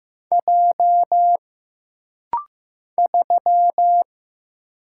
Morse code, heard as a keyed tone, about 15 wpm, 700 Hz. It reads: J 3
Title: Single Letter-Number Alt - Rapid-Fire 15wpm